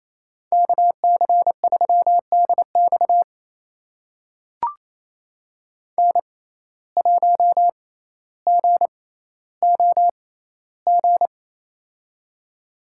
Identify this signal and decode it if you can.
Morse code, heard as a keyed tone, about 28 wpm, spaced as for 14 wpm, 700 Hz. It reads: KC3DX N1GOG
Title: USA Call Signs - 5 Characters - 6x Character Spacing - 28wpm